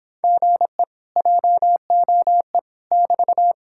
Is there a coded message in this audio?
Yes: Morse code from 0.2 to 3.5 s, heard as a keyed tone, about 26 wpm, 700 Hz. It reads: GE JOE <BT>